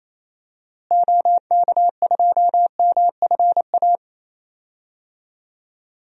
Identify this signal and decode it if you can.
Morse code, heard as a keyed tone, about 28 wpm, 700 Hz. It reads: OK2MFA
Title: DXCC-100 Call Signs - 6 Characters - Repeated 3x - 28wpm